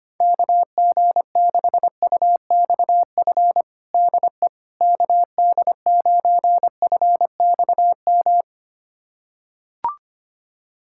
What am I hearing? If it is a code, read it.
Morse code, heard as a keyed tone, about 25 wpm, 700 Hz. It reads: KG6UXF DE KD9FXM
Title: USA Callsign DE Callsign 25wpm